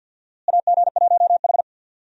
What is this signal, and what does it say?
Morse code, heard as a keyed tone, about 52 wpm, 700 Hz. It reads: AG1H
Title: USA Call Signs - 4 Characters - Repeated 2x - 50wpm